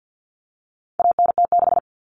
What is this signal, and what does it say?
Morse code, heard as a keyed tone, about 50 wpm, 700 Hz. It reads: ANT6
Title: Common Cut Numbers - 4-Digit 50wpm